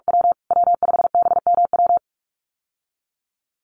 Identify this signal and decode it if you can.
Morse code, heard as a keyed tone, about 44 wpm, 700 Hz. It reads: W W5BMW
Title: USA Call Signs Repeated 2x 45wpm